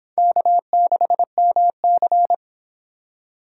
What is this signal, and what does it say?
Morse code, heard as a keyed tone, about 23 wpm, 700 Hz. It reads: K6MC